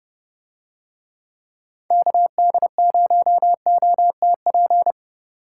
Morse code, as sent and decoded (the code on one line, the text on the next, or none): -.- -.. ----- --- - .--.
KD0OTP